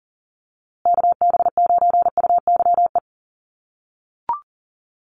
Morse code, heard as a keyed tone, about 40 wpm, 700 Hz. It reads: KB9UYE
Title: USA Call Signs Repeated 3x 40wpm